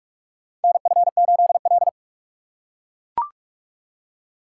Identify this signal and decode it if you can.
Morse code, heard as a keyed tone, about 44 wpm, 700 Hz. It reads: NF8L
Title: USA Call Signs - 4 Characters - 6x Character Spacing - 45wpm